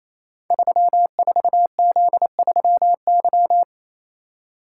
Morse code, heard as a keyed tone, about 27 wpm, 700 Hz. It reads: 34Z3Y
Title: Code Groups - 5 Letter:Numbers - 28wpm